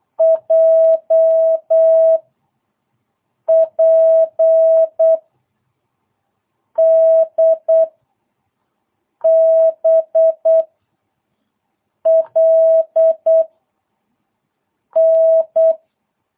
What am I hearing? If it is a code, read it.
Morse code, heard as a keyed tone, about 8 wpm, 650 Hz. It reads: J P D B L N
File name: jh8cbh / 2022cw